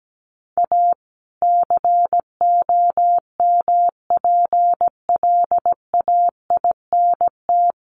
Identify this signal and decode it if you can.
Morse code, heard as a keyed tone, about 17 wpm, 700 Hz. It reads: A COMPLAINT
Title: Sentences from Top 1400 Words 17wpm